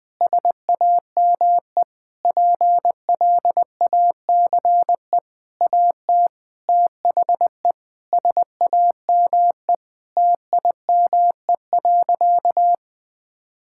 Morse code, heard as a keyed tone, about 20 wpm, 700 Hz. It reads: SAME PLACE AT THE SAME TIME.
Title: Sentences with 8-Word Phrases - 20wpm